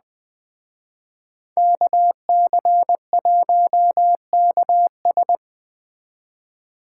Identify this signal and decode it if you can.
Morse code, heard as a keyed tone, about 20 wpm, 700 Hz. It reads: KC1KS
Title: Speed Racing - USA Call Signs 17wpm